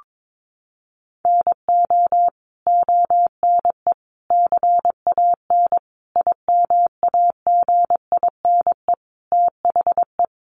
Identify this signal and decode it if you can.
Morse code, heard as a keyed tone, about 22 wpm, 700 Hz. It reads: NO ONE CAN IMAGINE THE
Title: Sentences from Top 1400 Words 22wpm